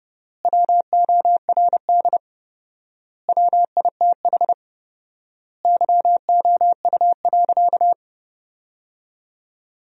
Morse code, heard as a keyed tone, about 30 wpm, 700 Hz. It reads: WORD WITH YOU.
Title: 4x Word-Spacing - Sentences from Top 300 Words - 30wpm